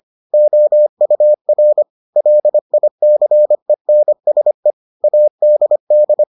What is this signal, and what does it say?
Morse code, heard as a keyed tone, about 25 wpm, 600 Hz. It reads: OUR LICENSE ADD